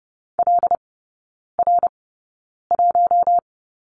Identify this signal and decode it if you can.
Morse code, heard as a keyed tone, about 30 wpm, spaced as for 9 wpm, 700 Hz. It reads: LR1
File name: Lesson 19e - Letter w - Three Characters 30wpm